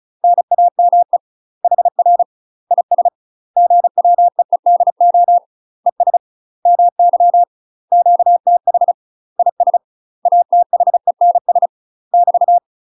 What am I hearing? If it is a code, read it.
Morse code, heard as a keyed tone, about 36 wpm, 700 Hz. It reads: NAME HR IS GWEEDO ES MY QTH IS ATHENS <BT>
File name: UK MASTER QSOs TRAINING / 35WPM / UK MASTER QSOs 0041